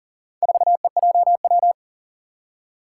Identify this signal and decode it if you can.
Morse code, heard as a keyed tone, about 39 wpm, 700 Hz. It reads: 4EJW
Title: Four Challenging Letter-Number Combinations 40wpm